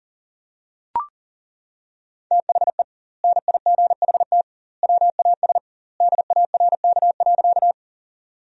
Morse code, heard as a keyed tone, about 40 wpm, 700 Hz. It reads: THE NIGHT WAS DARK.
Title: 2x Word-Spacing - Sentences from Top 500 Words - 40wpm